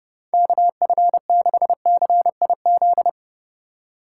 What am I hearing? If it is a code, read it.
Morse code, heard as a keyed tone, about 31 wpm, 700 Hz. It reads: KF6CIZ
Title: USA Call Signs - 6 Characters - 3x Character Spacing - 30wpm